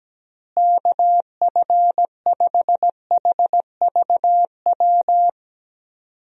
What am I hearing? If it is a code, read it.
Morse code, heard as a keyed tone, about 17 wpm, 700 Hz. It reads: KF5HVW
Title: USA Call Signs - 6 Characters - Repeated 3x - 17wpm